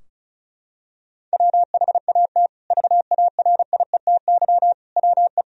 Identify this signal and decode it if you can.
Morse code, heard as a keyed tone, about 34 wpm, 700 Hz. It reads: WHAT VARIETY WE
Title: Sentences from Top 1000 Words - Review Part 4 - 35wpm